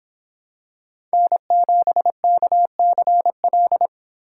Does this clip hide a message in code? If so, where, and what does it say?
Yes: Morse code from 1.1 to 3.9 s, heard as a keyed tone, about 25 wpm, 700 Hz. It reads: N7KCL